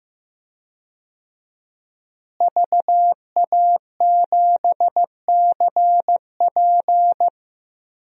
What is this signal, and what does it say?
Morse code, heard as a keyed tone, about 16 wpm, 700 Hz. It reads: VA7CP
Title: DXCC-100 Call Signs - 5 Characters - Repeated 3x - 15wpm